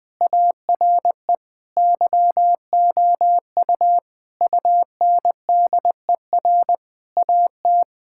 Morse code, heard as a keyed tone, about 20 wpm, 700 Hz. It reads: ARE YOU UNDER AT
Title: Sentences from Top 1000 Words - Review Part 6 - 20wpm